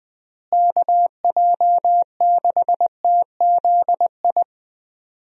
Morse code, heard as a keyed tone, about 21 wpm, 700 Hz. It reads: KJ6TZI
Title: USA Call Signs - 6 Characters - Repeated 2x - 20wpm